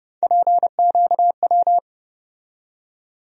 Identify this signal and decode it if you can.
Morse code, heard as a keyed tone, about 30 wpm, 700 Hz. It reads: PQW
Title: USA Call Sign Postfixes - 3 Letters 30wpm